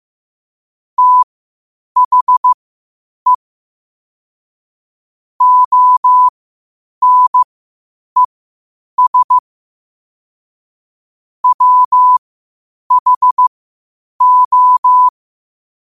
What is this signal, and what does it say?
Morse code, heard as a keyed tone, about 15 wpm, spaced as for 8 wpm, 1000 Hz. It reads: THE ONES WHO
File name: qotdfiles / 8wpm